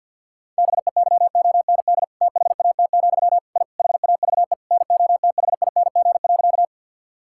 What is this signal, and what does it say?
Morse code, heard as a keyed tone, about 50 wpm, 700 Hz. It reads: BEYOND THAT, I HAVE NOTHING.
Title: Sentences from Top 1000 Words - Review Part 6 - 50wpm